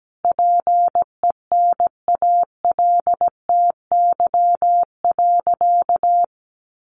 Morse code, heard as a keyed tone, about 17 wpm, 700 Hz. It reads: PENALTY.
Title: A Sound Of Thunder 17wpm